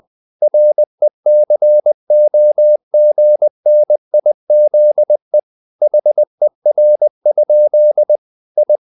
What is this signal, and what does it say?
Morse code, heard as a keyed tone, about 20 wpm, 600 Hz. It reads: RECOGNIZE HER? I
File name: the old man and the sea part 2 20wpm